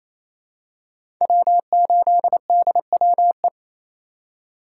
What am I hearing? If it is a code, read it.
Morse code, heard as a keyed tone, about 28 wpm, 700 Hz. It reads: W8DWE
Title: USA Call Signs - 5 Characters - 4x Character Spacing - 28wpm